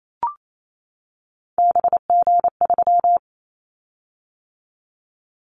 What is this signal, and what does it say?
Morse code, heard as a keyed tone, about 28 wpm, 700 Hz. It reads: BG3
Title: Callsign Prefixes for Top 100 DXCC - 28wpm